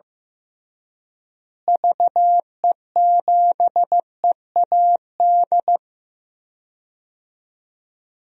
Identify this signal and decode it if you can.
Morse code, heard as a keyed tone, about 15 wpm, 700 Hz. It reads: VE7EAD
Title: DXCC-100 Call Signs - 6 Characters - Repeated 3x - 15wpm